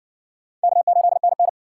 Morse code, heard as a keyed tone, about 53 wpm, 700 Hz. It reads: K8NN